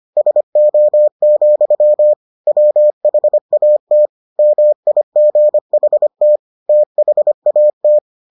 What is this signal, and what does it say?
Morse code, heard as a keyed tone, about 25 wpm, 600 Hz. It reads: SO, WHAT MIGHT THAT